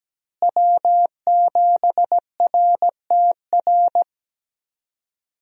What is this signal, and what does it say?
Morse code, heard as a keyed tone, about 17 wpm, 700 Hz. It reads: W7RTR
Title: USA Call Signs - 5 Characters - Repeated 3x - 17wpm